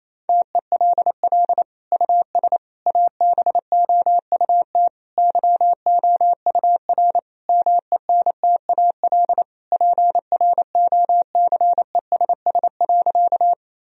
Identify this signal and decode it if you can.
Morse code, heard as a keyed tone, about 28 wpm, 700 Hz. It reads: TELL US ABOUT YOUR MENTAL PROCESS.
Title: Sentences from Top 1000 Words - Review Part 4 - 28wpm